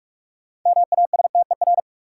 Morse code, heard as a keyed tone, about 45 wpm, 700 Hz. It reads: MASTER
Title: Sentences from Top 1300 Words - New Word Drill Alt - 45wpm